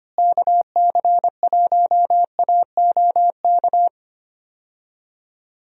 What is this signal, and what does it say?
Morse code, heard as a keyed tone, about 25 wpm, 700 Hz. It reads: KC1AOK
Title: USA Call Signs - 6 Characters - Repeated 2x - 25wpm